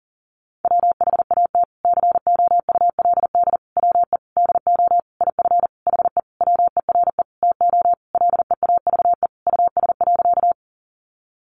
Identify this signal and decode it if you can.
Morse code, heard as a keyed tone, about 40 wpm, 700 Hz. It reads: WHAT COULD WE DO IF HE WERE TO LEAVE US.
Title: Sentences from Top 400 Words 40wpm